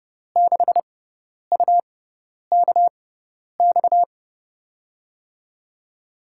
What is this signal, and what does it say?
Morse code, heard as a keyed tone, about 30 wpm, spaced as for 10 wpm, 700 Hz. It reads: 6UKX
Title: USA Call Signs - 5 Characters - 6x Character Spacing - 30wpm